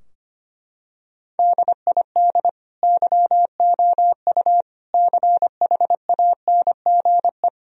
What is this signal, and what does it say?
Morse code, heard as a keyed tone, about 25 wpm, 700 Hz. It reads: DID YOU CHANGE
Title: Sentences from Top 1000 Words - Review Part 2 - 25wpm